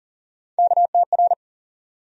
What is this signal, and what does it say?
Morse code, heard as a keyed tone, about 40 wpm, 700 Hz. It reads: KTR
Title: USA Call Sign Postfixes - 3 Letters 40wpm